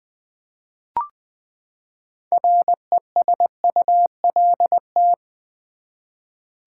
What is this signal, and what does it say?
Morse code, heard as a keyed tone, about 20 wpm, 700 Hz. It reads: RESULT
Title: Six Letter Words - 20wpm